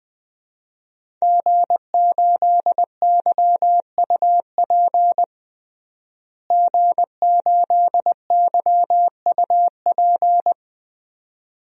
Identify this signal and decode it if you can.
Morse code, heard as a keyed tone, about 21 wpm, 700 Hz. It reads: G8YUP G8YUP
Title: DXCC-100 Call Signs - 5 Characters - Repeated 2x - 20wpm